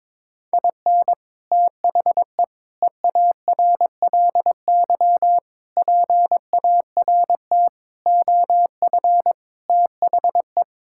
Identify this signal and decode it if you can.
Morse code, heard as a keyed tone, about 21 wpm, 700 Hz. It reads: IN THE EARLY PART OF THE
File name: practice / 22wpm / Sets of 6 Words - 22wpm